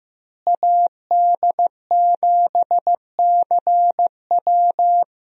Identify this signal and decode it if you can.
Morse code, heard as a keyed tone, about 15 wpm, 700 Hz. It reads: AD7CW
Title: USA Call Signs - 5 Characters - Repeated 3x - 15wpm